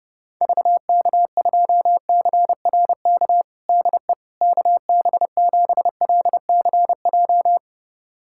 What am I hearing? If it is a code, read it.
Morse code, heard as a keyed tone, about 30 wpm, 700 Hz. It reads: VK2CRK DE KB7LCJ
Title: USA Callsign DE Callsign 30wpm